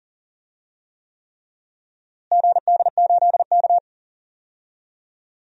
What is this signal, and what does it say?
Morse code, heard as a keyed tone, about 37 wpm, 700 Hz. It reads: GD8K
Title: DXCC-100 Call Signs - 4 Characters - Repeated 3x - 40wpm